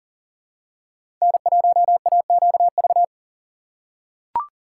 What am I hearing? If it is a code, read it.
Morse code, heard as a keyed tone, about 40 wpm, 700 Hz. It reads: N1AQV
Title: USA Call Signs - 2x Character Spacing - 40wpm